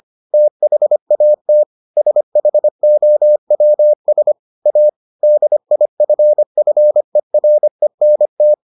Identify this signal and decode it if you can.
Morse code, heard as a keyed tone, about 25 wpm, 600 Hz. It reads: THAT SHOWS A DIFFERENT